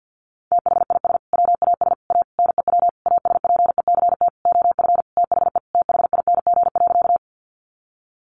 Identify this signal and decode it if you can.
Morse code, heard as a keyed tone, about 50 wpm, 700 Hz. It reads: THIS WAS A NEW ASPECT OF THE THING.
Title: Sentences from Top 1300 Words 50wpm